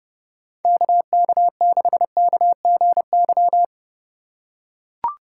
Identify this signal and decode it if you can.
Morse code, heard as a keyed tone, about 31 wpm, 700 Hz. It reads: KK6KGY